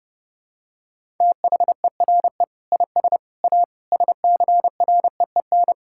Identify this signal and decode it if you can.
Morse code, heard as a keyed tone, about 30 wpm, 700 Hz. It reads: THERE IS A SCREEN
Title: Sentences from Top 1000 Words - Review Part 6 - 30wpm